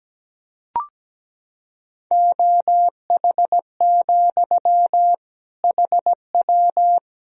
Morse code, heard as a keyed tone, about 17 wpm, 700 Hz. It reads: OH, HW
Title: Sentences from Top 500 Words and Abbreviations 17wpm